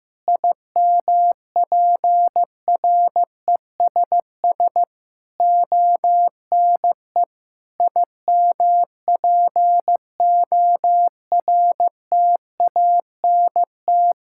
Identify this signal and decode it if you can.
Morse code, heard as a keyed tone, about 16 wpm, 700 Hz. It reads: IMPRESS ONE IMPORTANT
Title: Sentences from Top 1600 Words 15wpm